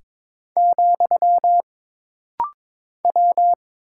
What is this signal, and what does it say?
Morse code, heard as a keyed tone, about 22 wpm, 700 Hz. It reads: , W
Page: This rapid-fire timing is geared to push your ICR (Instant Character Recognition) skills.